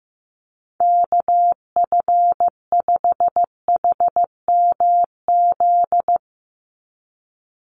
Morse code, heard as a keyed tone, about 15 wpm, 700 Hz. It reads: KF5HMZ
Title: USA Call Signs 15wpm